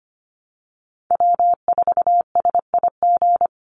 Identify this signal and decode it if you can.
Morse code, heard as a keyed tone, about 25 wpm, 700 Hz. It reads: W4SIG
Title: Call Signs Repeated 3x - 25wpm